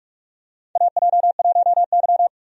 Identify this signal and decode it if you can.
Morse code, heard as a keyed tone, about 43 wpm, 700 Hz. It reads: AJ1Y
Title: USA Call Signs - 4 Characters - 45wpm